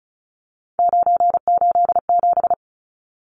Morse code, heard as a keyed tone, about 34 wpm, 700 Hz. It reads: 987
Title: Numbers - 3-Digit 35wpm